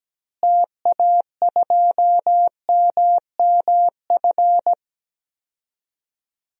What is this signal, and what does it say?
Morse code, heard as a keyed tone, about 17 wpm, 700 Hz. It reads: TA2MMF
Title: DXCC-100 Call Signs - 6 Characters - Repeated 3x - 17wpm